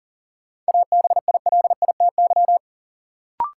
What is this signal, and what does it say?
Morse code, heard as a keyed tone, about 39 wpm, 700 Hz. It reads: ABILITY
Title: Sets of 3 Words - New Word Drill - 40wpm